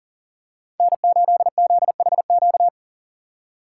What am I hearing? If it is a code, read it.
Morse code, heard as a keyed tone, about 40 wpm, 700 Hz. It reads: N8ZHQ
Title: USA Call Signs - 3x Character Spacing - 40wpm